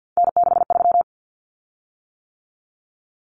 Morse code, heard as a keyed tone, about 50 wpm, 700 Hz. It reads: N63
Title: Common Cut Numbers - 3-Digit 50wpm